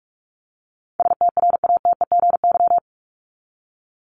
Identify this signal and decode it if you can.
Morse code, heard as a keyed tone, about 44 wpm, 700 Hz. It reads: STRATEGY
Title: Top 501-750 Words 45wpm